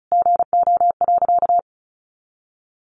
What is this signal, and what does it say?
Morse code, heard as a keyed tone, about 34 wpm, 700 Hz. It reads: GO.
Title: Sentences from Top 300 Words 35wpm